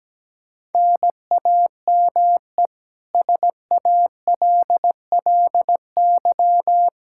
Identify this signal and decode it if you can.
Morse code, heard as a keyed tone, about 17 wpm, 700 Hz. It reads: NAME SALLY
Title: K1USN SST QSO Practice 17wpm